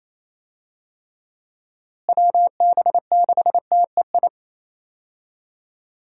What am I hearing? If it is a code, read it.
Morse code, heard as a keyed tone, about 27 wpm, 700 Hz. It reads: WB6TEI
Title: USA Call Signs - 6 Characters - Repeated 2x - 28wpm